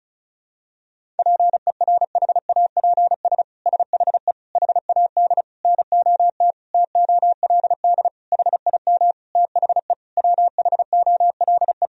Morse code, heard as a keyed tone, about 34 wpm, 700 Hz. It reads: PERHAPS SHE HAD NOT TOLD HIM THE WHOLE
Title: Sentences from Top 1000 Words - Review Part 1 - 35wpm